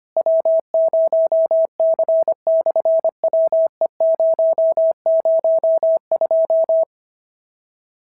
Morse code, heard as a keyed tone, about 25 wpm, 650 Hz. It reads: W0C/WE002
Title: SOTA QSO Practice 25wpm